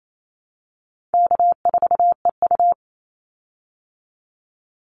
Morse code, heard as a keyed tone, about 28 wpm, 700 Hz. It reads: K4EU
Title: Easy Call Signs Repeated 2x - 28wpm